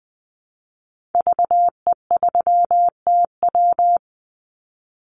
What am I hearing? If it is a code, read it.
Morse code, heard as a keyed tone, about 20 wpm, 700 Hz. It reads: VE3TW
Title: Call Signs - 20wpm